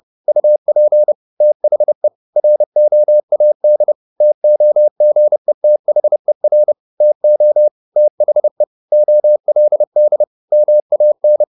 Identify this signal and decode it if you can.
Morse code, heard as a keyed tone, about 30 wpm, 600 Hz. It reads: UP THE ROAD TOGETHER TO THE OLD MAN
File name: the old man and the sea / the old man and the sea 1 30wpm